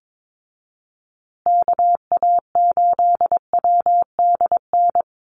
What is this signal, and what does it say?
Morse code, heard as a keyed tone, about 22 wpm, 700 Hz. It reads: KA8WDN
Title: USA Call Signs Repeated 3x 22wpm